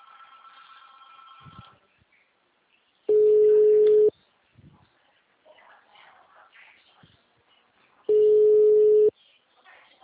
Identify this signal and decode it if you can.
Ringback tone (Europe, 425 Hz), with speech.
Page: Itu - SP